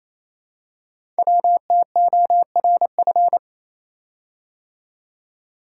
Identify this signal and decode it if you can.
Morse code, heard as a keyed tone, about 28 wpm, 700 Hz. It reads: WTORF
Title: Code Groups - 5 Letter:Numbers - 28wpm